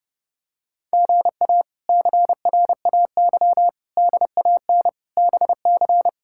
Morse code, heard as a keyed tone, about 30 wpm, 700 Hz. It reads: GA CRAY DAN BC
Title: K1USN SST QSO Practice - 30wpm